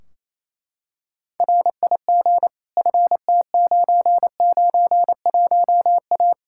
Morse code, heard as a keyed tone, about 28 wpm, 700 Hz. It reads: RIG FT991A
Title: Practice QSO - Exchange 2 - Info - 28wpm